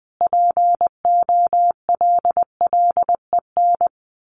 Morse code, heard as a keyed tone, about 20 wpm, 700 Hz. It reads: POLLEN
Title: A Sound Of Thunder 20wpm